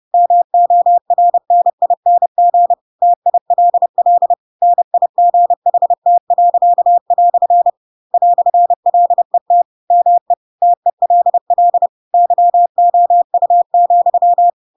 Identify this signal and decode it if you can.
Morse code, heard as a keyed tone, about 30 wpm, 700 Hz. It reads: MORNING TILL NIGHT." "LET ME TELL YOU,
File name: Pinocchio / Pinocchio ch4 30 30 ik1hge